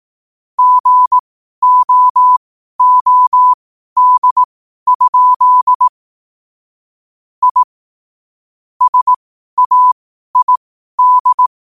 Morse code, heard as a keyed tone, about 18 wpm, spaced as for 12 wpm, 1000 Hz. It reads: GOOD? I SAID
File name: qotdfiles / 12wpm